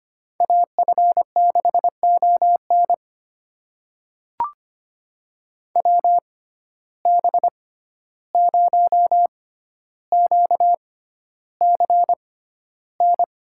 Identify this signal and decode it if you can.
Morse code, heard as a keyed tone, about 25 wpm, spaced as for 12 wpm, 700 Hz. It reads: AF6ON WB0QCN
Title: USA Call Signs - 6x Character Spacing - 25wpm